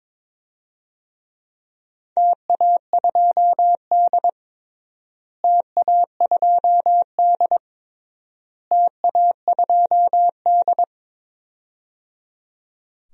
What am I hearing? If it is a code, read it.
Morse code, heard as a keyed tone, about 22 wpm, 700 Hz. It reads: TA2D TA2D TA2D
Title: DXCC-100 Call Signs - 4 Characters - Repeated 3x - 22wpm